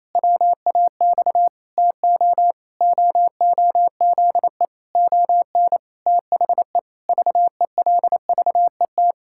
Morse code, heard as a keyed tone, about 28 wpm, 700 Hz. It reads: WAX TO OOZE ON THE VELVET
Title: Pangram Challenge v2 - 28wpm